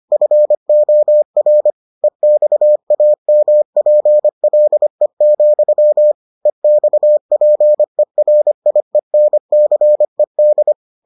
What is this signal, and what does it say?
Morse code, heard as a keyed tone, about 25 wpm, 600 Hz. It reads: FOR EXAMPLE, EXPERIENCED